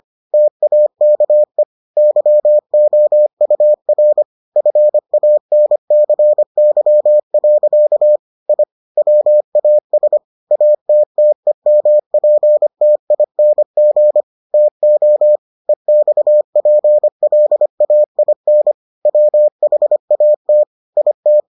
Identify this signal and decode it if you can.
Morse code, heard as a keyed tone, about 25 wpm, 600 Hz. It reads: TAKE YOUR FANCY. I WAS ATTEMPTING TO EXPLAIN WHAT IT